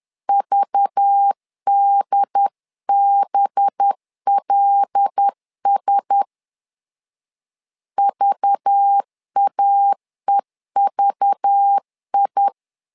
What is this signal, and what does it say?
Morse code, heard as a keyed tone, about 10 wpm, 800 Hz. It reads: VDBLS VAEVI
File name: LEZIONI / 5 LBVC / 5 LBV 10wpm 2248